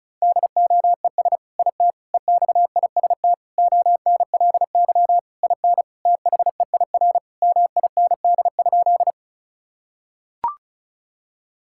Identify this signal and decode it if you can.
Morse code, heard as a keyed tone, about 36 wpm, 700 Hz. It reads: DOES IT EXIST ONLY IN THEIR MIND?
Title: Sentences from Top 1000 Words - Review Part 5 - 35wpm